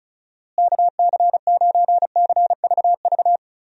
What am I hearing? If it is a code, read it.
Morse code, heard as a keyed tone, about 34 wpm, 700 Hz. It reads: KC9CVV
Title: USA Call Signs - 6 Characters - Repeated 3x - 35wpm